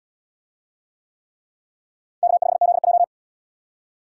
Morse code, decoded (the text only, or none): K5ZJ